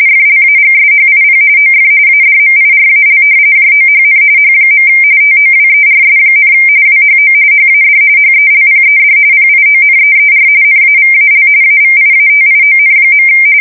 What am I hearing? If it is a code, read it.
RTTY, about 45.45 baud. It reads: FLAGSHIP OF ADMIRAL JOHN H. TOWERS, WHO HAD HELPED PIONEER NAVAL AVIATION.
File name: fsk170 / battleshipnj